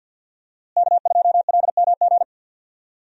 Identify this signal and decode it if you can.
Morse code, heard as a keyed tone, about 52 wpm, 700 Hz. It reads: K2LMG
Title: USA Call Signs - 5 Characters - Repeated 3x - 50wpm